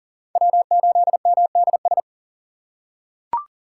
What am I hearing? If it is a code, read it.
Morse code, heard as a keyed tone, about 40 wpm, 700 Hz. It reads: W8MDS
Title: USA Call Signs - 5 Characters - 4x Character Spacing - 40wpm